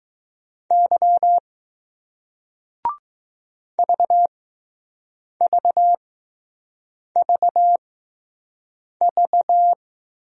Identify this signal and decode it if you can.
Morse code, heard as a keyed tone, about 20 wpm, spaced as for 6 wpm, 700 Hz. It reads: Y VVVV